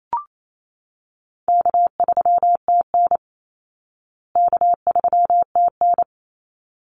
Morse code, heard as a keyed tone, about 28 wpm, 700 Hz. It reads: K3TN K3TN
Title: Easy Call Signs Repeated 2x - 28wpm